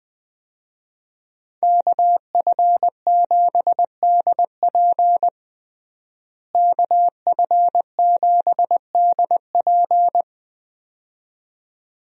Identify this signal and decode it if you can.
Morse code, heard as a keyed tone, about 20 wpm, 700 Hz. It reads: KF7DP KF7DP
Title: USA Call Signs - 5 Characters - Repeated 2x - 20wpm